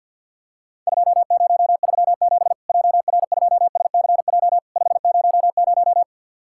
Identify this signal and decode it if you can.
Morse code, heard as a keyed tone, about 48 wpm, 700 Hz. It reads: 2037 JR2SCJ 500
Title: Worked All Europe DX Contest - QTC Exchange - 50wpm